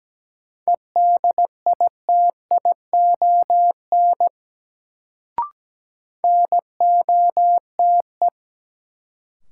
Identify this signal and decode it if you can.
Morse code, heard as a keyed tone, about 17 wpm, 700 Hz. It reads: EDITION NOTE
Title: Sets of 5 Words - New Word Drill - 17wpm